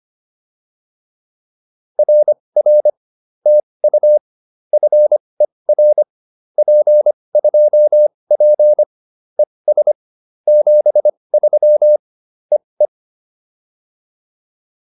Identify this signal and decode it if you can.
Morse code, heard as a keyed tone, about 25 wpm, spaced as for 20 wpm, 600 Hz. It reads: RR TU FER P2P ES 73 EE